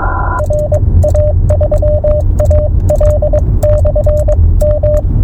Morse code, heard as a keyed tone, about 22 wpm, spaced as for 19 wpm, 600 Hz, with radio noise. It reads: RA3AL/M